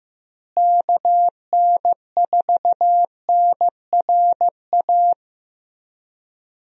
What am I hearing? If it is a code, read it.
Morse code, heard as a keyed tone, about 15 wpm, 700 Hz. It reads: KN4NRA
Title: USA Call Signs - 6 Characters - Repeated 3x - 15wpm